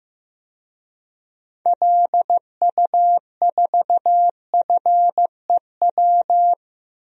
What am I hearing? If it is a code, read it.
Morse code, heard as a keyed tone, about 15 wpm, 700 Hz. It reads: LU4FEW
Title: DXCC-100 Call Signs - 6 Characters - 15wpm